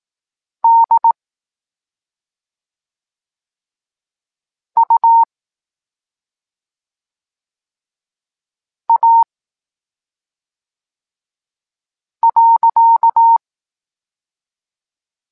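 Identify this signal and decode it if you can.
Morse code, heard as a keyed tone, about 18 wpm, 920 Hz. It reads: D U A .
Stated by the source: Evenly spaced Morse lesson text 21 for learning letters G and Z, from SP9MOA Niepołomicki Klub Krótkofalowców.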